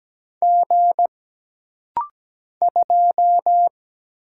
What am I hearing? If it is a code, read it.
Morse code, heard as a keyed tone, about 17 wpm, 700 Hz. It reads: G 2
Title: Single Letter-Number - Rapid-Fire 17wpm